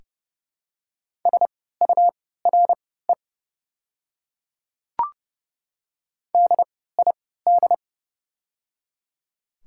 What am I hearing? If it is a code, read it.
Morse code, heard as a keyed tone, about 30 wpm, spaced as for 17 wpm, 700 Hz. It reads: SURE DID
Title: 3x Character Spacing - Top 300 Words - 30wpm